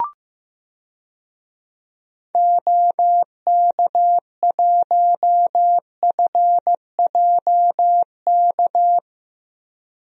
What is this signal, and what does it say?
Morse code, heard as a keyed tone, about 15 wpm, 700 Hz. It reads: OK1FJK
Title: DXCC-100 Call Signs - 6 Characters - Repeated 3x - 15wpm